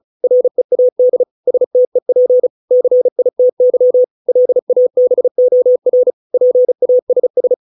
Morse code, heard as a keyed tone, about 34 wpm, 500 Hz. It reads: READ STEP CITY LABOR PASS